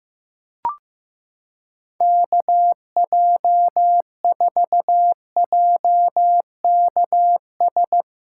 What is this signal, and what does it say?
Morse code, heard as a keyed tone, about 16 wpm, 700 Hz. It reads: KJ4JKS
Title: USA Callsign DE Callsign - 6 Characters - 15wpm